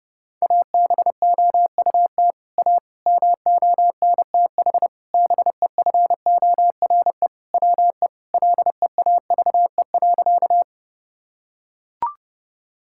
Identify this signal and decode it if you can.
Morse code, heard as a keyed tone, about 30 wpm, 700 Hz. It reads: ABOUT A MONTH BEFORE WE LEAVE.
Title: Sentences from Top 1000 Words - Review Part 5 - 30wpm